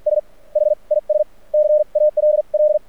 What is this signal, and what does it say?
Morse code, heard as a keyed tone, about 48 wpm, spaced as for 33 wpm, 600 Hz. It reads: SHIS CALL
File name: archived / CRSnet